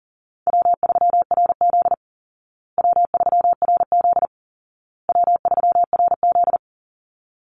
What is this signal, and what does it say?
Morse code, heard as a keyed tone, about 40 wpm, 700 Hz. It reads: W3RZ W3RZ W3RZ
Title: Easy Call Signs Repeated 3x - 40wpm